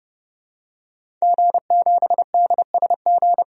Morse code, heard as a keyed tone, about 30 wpm, 700 Hz. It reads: G7DSG
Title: DXCC-100 Call Signs - 5 Characters - 30wpm